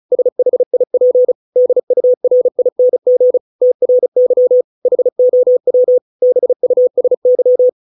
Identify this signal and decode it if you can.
Morse code, heard as a keyed tone, about 34 wpm, 500 Hz. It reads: SHIP DURING TRY HOW BUSY